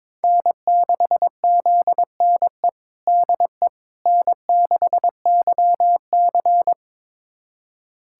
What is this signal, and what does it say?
Morse code, heard as a keyed tone, about 22 wpm, 700 Hz. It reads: N6ZNE DE N6YC
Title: USA Callsign DE Callsign 22wpm